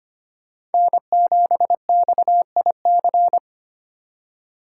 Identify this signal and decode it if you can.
Morse code, heard as a keyed tone, about 32 wpm, 700 Hz. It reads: N7XIC